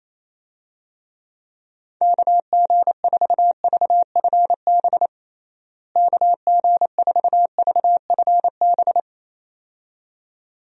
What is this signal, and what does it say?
Morse code, heard as a keyed tone, about 28 wpm, 700 Hz. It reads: KG4VFB KG4VFB
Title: USA Call Signs - 6 Characters - Repeated 2x - 28wpm